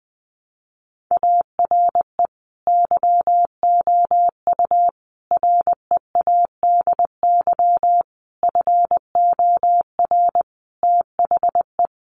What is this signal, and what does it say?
Morse code, heard as a keyed tone, about 20 wpm, 700 Hz. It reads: ARE YOU READY FOR THE
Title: Sentences from Top 1300 Words 20wpm